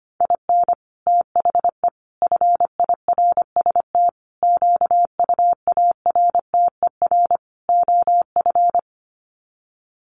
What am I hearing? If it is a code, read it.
Morse code, heard as a keyed tone, about 25 wpm, 700 Hz. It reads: IN THE FIRST QUARTER OF
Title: Sets of 5 Words Encore2 25wpm